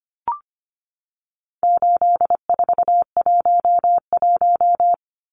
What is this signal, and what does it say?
Morse code, heard as a keyed tone, about 25 wpm, 700 Hz. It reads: 8411
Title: Numbers - 4-Digit 25wpm